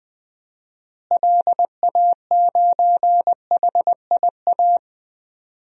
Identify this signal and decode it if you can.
Morse code, heard as a keyed tone, about 21 wpm, 700 Hz. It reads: LA9HIA
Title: DXCC-100 Call Signs - 6 Characters - Repeated 3x - 20wpm